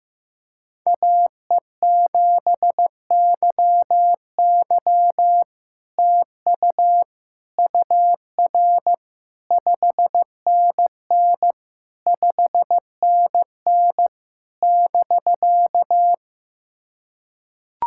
Morse code, heard as a keyed tone, about 15 wpm, 700 Hz. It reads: AE7YY TU UR 5NN 5NN <BK>
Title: SOTA QSO Practice 15wpm